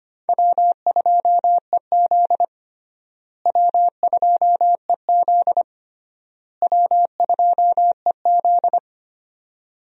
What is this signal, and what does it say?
Morse code, heard as a keyed tone, about 25 wpm, 700 Hz. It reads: W2EZ W2EZ W2EZ
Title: USA Call Signs - 4 Characters - Repeated 3x - 25wpm